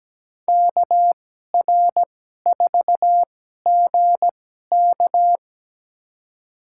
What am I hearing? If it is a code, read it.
Morse code, heard as a keyed tone, about 17 wpm, spaced as for 12 wpm, 700 Hz. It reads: KR4GK
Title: USA Call Signs - 5 Characters - 2x Character Spacing - 17wpm